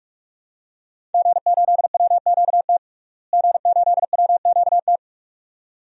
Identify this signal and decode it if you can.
Morse code, heard as a keyed tone, about 42 wpm, 700 Hz. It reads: G8WQT G8WQT
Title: DXCC-100 Call Signs - 5 Characters - Repeated 2x - 45wpm